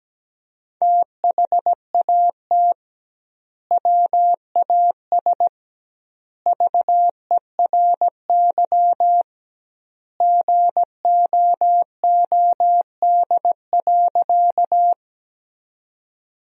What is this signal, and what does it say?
Morse code, heard as a keyed tone, about 17 wpm, 700 Hz. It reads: THAT WAS VERY GOOD.
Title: 2x Word-Spacing - Sentences from Top 200 Words - 17wpm